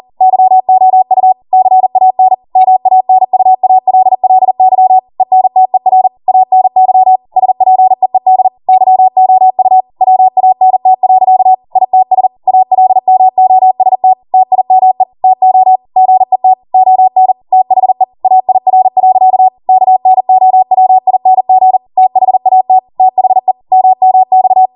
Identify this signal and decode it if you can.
Morse code, heard as a keyed tone, about 40 wpm, 750 Hz. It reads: YOU CAN MANUALLY ENTER ANY SPEED YOU WANT. ITS ALMOST TIME TO GET ON THE AIR. KNOWING THAT THE MMX